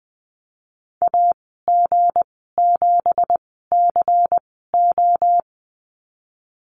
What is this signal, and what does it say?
Morse code, heard as a keyed tone, about 20 wpm, spaced as for 15 wpm, 700 Hz. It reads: AG7CO
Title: 186 Lesson 30g - Letter b - Call Signs 20wpm